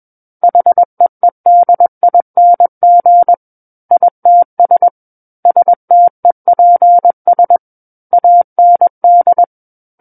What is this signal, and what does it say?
Morse code, heard as a keyed tone, about 21 wpm, 700 Hz. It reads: HEEDING ITS STEPS AND
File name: Book One / 12-Chapter 12 - 21 WPM